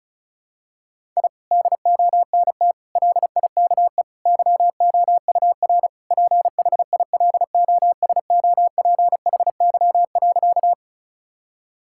Morse code, heard as a keyed tone, about 36 wpm, 700 Hz. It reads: I DONT LIKE YOUR PHILOSOPHY.
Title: Sentences from Top 1000 Words - Review Part 6 - 35wpm